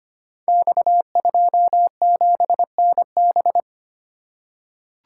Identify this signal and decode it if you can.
Morse code, heard as a keyed tone, about 25 wpm, 700 Hz. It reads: X27NB
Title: Code Groups - 5 Letter:Numbers - 25wpm